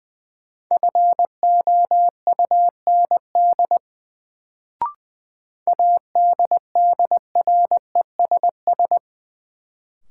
Morse code, heard as a keyed tone, about 20 wpm, 700 Hz. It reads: FOUND ADDRESS
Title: Sets of 5 Words Encore - New Word Drill - 20wpm